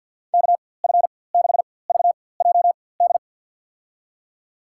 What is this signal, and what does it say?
Morse code, heard as a keyed tone, about 48 wpm, spaced as for 23 wpm, 700 Hz. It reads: KF6VJD
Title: USA Call Signs - 6 Characters - 4x Character Spacing - 50wpm